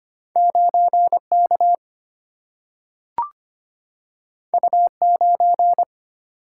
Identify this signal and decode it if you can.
Morse code, heard as a keyed tone, about 25 wpm, 700 Hz. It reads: 9K U9
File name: Two Letter-Number Combinations - 25wpm